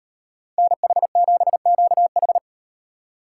Morse code, heard as a keyed tone, about 38 wpm, 700 Hz. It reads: NH7QH